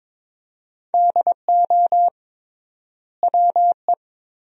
Morse code, heard as a keyed tone, about 21 wpm, 700 Hz. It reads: DO WE
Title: 3x Word-Spacing - Sentences from Top 300 Words - 22wpm